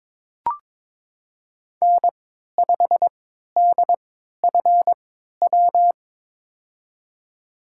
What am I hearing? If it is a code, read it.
Morse code, heard as a keyed tone, about 22 wpm, spaced as for 12 wpm, 700 Hz. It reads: N5DFW
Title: USA Call Signs - 3x Character Spacing - 22wpm